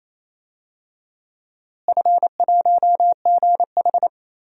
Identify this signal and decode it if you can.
Morse code, heard as a keyed tone, about 27 wpm, 700 Hz. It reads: F1GH
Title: DXCC-100 Call Signs - 4 Characters - Repeated 2x - 28wpm